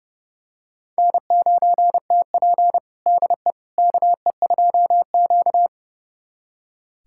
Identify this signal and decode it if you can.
Morse code, heard as a keyed tone, about 30 wpm, 700 Hz. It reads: N9TP DE KE2Q
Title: USA Callsign DE Callsign - 4 Characters - 30wpm